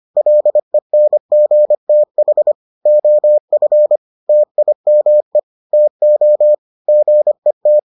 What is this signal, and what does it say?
Morse code, heard as a keyed tone, about 25 wpm, 600 Hz. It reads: LENGTH OF TIME TO GET